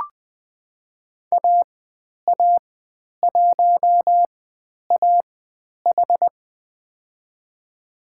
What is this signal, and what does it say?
Morse code, heard as a keyed tone, about 20 wpm, spaced as for 10 wpm, 700 Hz. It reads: AA1AH
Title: 095 Lesson 16g - Letter c - Call Signs 20wpm